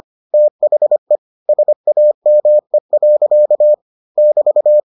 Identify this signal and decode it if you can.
Morse code, heard as a keyed tone, about 25 wpm, 600 Hz. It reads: THE SAME. <BT>